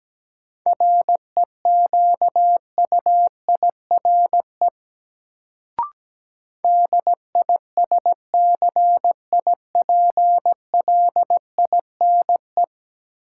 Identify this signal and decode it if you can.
Morse code, heard as a keyed tone, about 17 wpm, 700 Hz. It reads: REQUIRE DISCIPLINE
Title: Sentences from Top 900 Words - New Word Drill - 17wpm